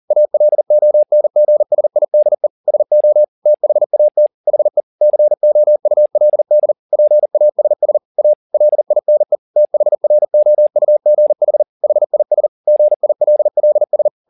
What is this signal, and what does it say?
Morse code, heard as a keyed tone, about 40 wpm, 600 Hz. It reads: ALONGSIDE SO THAT HE COULD PASS A LINE THROUGH HIS GILLS
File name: the old man and the sea 2 40wpm